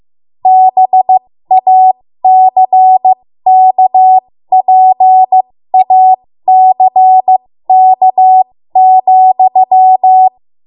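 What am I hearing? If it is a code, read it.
Morse code, heard as a keyed tone, about 15 wpm, spaced as for 13 wpm, 750 Hz. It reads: BACKPACK,
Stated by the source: ARRL Morse code practice at 13 WPM.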